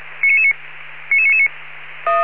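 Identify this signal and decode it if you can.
Five-tone selective call (ZVEI), heard as quick runs of five tones, tones of about 70 ms. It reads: 99999
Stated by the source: ZVEI five-tone selective call received on an SDR.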